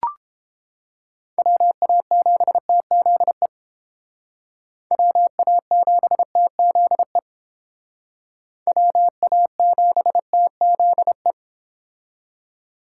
Morse code, heard as a keyed tone, about 31 wpm, 700 Hz. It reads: WA7TZE WA7TZE WA7TZE